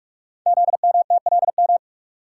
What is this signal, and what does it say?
Morse code, heard as a keyed tone, about 44 wpm, 700 Hz. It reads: ZMTLM
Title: Code Groups - 5 Letters - 45wpm